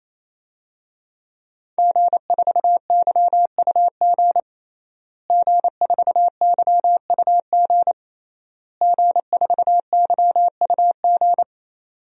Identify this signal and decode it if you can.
Morse code, heard as a keyed tone, about 27 wpm, 700 Hz. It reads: G4YUG G4YUG G4YUG
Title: DXCC-100 Call Signs - 5 Characters - Repeated 3x - 28wpm